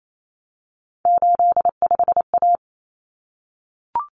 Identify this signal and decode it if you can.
Morse code, heard as a keyed tone, about 28 wpm, 700 Hz. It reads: 85A
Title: Common Cut Numbers - 3-Digit 28wpm